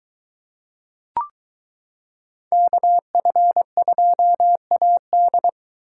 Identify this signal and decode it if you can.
Morse code, heard as a keyed tone, about 22 wpm, 700 Hz. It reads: KF2AD